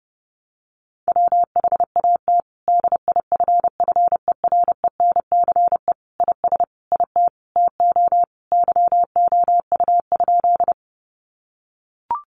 Morse code, heard as a keyed tone, about 30 wpm, 700 Hz. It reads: WHAT DIFFERENCE IS IT TO YOU?
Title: Sentences from Top 800 Words 30wpm